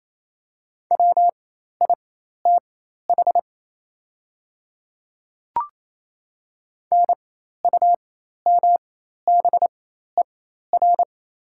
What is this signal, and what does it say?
Morse code, heard as a keyed tone, about 27 wpm, spaced as for 13 wpm, 700 Hz. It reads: WITH NUMBER
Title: 4x Character Spacing - Top 100 Words - 28wpm